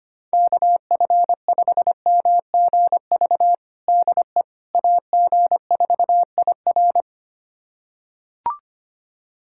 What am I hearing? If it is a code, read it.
Morse code, heard as a keyed tone, about 25 wpm, 700 Hz. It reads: KF5MGV DE AG4IR
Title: USA Callsign DE Callsign 25wpm